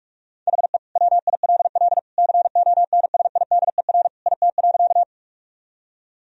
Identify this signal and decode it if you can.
Morse code, heard as a keyed tone, about 41 wpm, 700 Hz. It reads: HE WILL CONSIDER IT.
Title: Sentences from Top 1000 Words - Review Part 4 - 45wpm